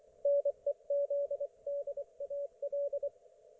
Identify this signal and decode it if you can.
Morse code, heard as a keyed tone, about 24 wpm, 570 Hz. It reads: NEZDAL
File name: Archiv / CUC224 / ok1fhi cul224